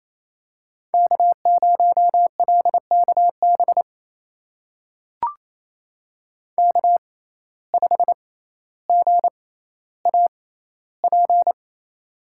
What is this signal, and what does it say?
Morse code, heard as a keyed tone, about 27 wpm, spaced as for 14 wpm, 700 Hz. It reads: K0LKB K5GAP
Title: USA Call Signs - 5 Characters - 6x Character Spacing - 28wpm